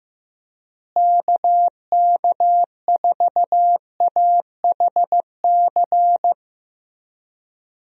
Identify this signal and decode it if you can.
Morse code, heard as a keyed tone, about 15 wpm, 700 Hz. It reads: KK4AHC
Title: USA Call Signs - 6 Characters - Repeated 3x - 15wpm